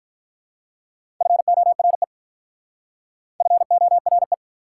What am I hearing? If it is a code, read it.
Morse code, heard as a keyed tone, about 46 wpm, 690 Hz. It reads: FORE FORE